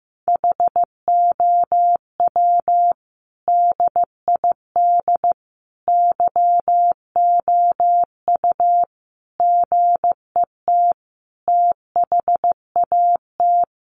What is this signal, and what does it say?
Morse code, heard as a keyed tone, about 15 wpm, 700 Hz. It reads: HOW DID YOU GET THAT
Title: Sentences from Top 900 Words 15wpm